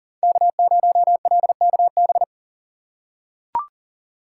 Morse code, heard as a keyed tone, about 40 wpm, 700 Hz. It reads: K0LKB
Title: USA Call Signs - 4x Character Spacing - 40wpm